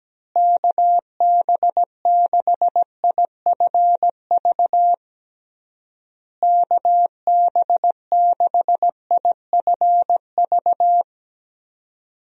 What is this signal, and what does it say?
Morse code, heard as a keyed tone, about 17 wpm, 700 Hz. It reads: KB6IFV KB6IFV
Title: USA Call Signs - 6 Characters - Repeated 3x - 17wpm